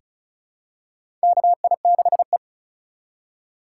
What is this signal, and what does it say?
Morse code, heard as a keyed tone, about 34 wpm, 700 Hz. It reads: KI6E
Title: USA Call Signs - 4 Characters - 3x Character Spacing - 35wpm